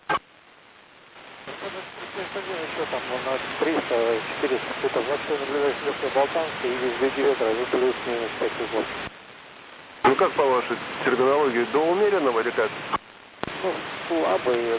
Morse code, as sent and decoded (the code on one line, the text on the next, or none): none